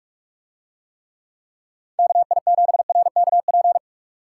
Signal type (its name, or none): Morse code